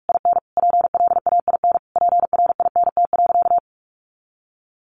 Morse code, heard as a keyed tone, about 44 wpm, 700 Hz. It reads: IN PLAIN PRINT.
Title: Sentences from Top 1400 Words 45wpm